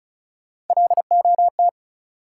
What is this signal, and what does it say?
Morse code, heard as a keyed tone, about 36 wpm, 700 Hz. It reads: LOT
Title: Sentences from Top 500 Words - New Word Drill - 35wpm